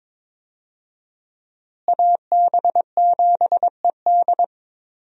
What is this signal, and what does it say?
Morse code, heard as a keyed tone, about 22 wpm, 700 Hz. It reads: AB7ED
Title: USA Call Signs - 5 Characters - Repeated 3x - 22wpm